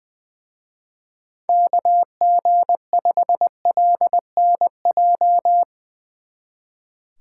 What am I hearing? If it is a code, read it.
Morse code, heard as a keyed tone, about 22 wpm, 700 Hz. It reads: KG5LNJ
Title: Speed Racing - USA Call Signs 20wpm